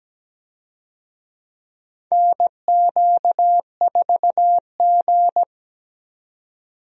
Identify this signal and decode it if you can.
Morse code, heard as a keyed tone, about 17 wpm, 700 Hz. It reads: NQ4G
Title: USA Call Signs - 4 Characters - 17wpm